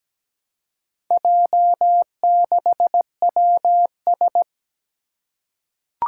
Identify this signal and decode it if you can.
Morse code, heard as a keyed tone, about 17 wpm, 700 Hz. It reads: J6WS
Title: Four Challenging Letter-Number Combinations 17wpm